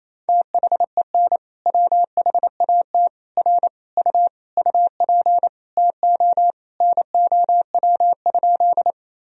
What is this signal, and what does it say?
Morse code, heard as a keyed tone, about 28 wpm, 700 Hz. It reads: THEN WHAT R U UP TO NOW?
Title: Sentences from Top 100 Words and Abbreviations 28wpm